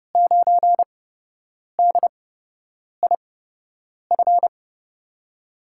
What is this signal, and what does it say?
Morse code, heard as a keyed tone, about 30 wpm, spaced as for 8 wpm, 700 Hz. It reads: 9DIF
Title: USA Call Signs - 5 Characters - 8x Character Spacing - 30wpm